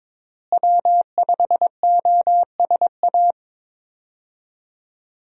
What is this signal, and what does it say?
Morse code, heard as a keyed tone, about 22 wpm, 700 Hz. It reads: W5OSA
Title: USA Call Signs - 5 Characters - Repeated 2x - 22wpm